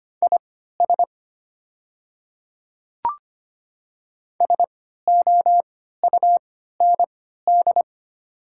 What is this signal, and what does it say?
Morse code, heard as a keyed tone, about 25 wpm, spaced as for 14 wpm, 700 Hz. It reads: IS SOUND
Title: 3x Character Spacing - Top 100 Words - 25wpm